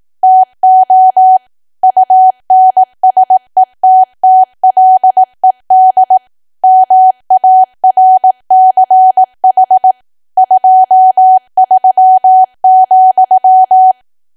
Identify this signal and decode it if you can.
Morse code, heard as a keyed tone, about 18 wpm, 750 Hz. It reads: TO UNSETTLED MARCH 23,
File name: cwa / W1AW-Level2 / 18WPM